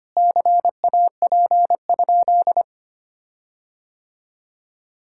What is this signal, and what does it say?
Morse code, heard as a keyed tone, about 25 wpm, 700 Hz. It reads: CAP?
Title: Sentences from Top 1600 Words 25wpm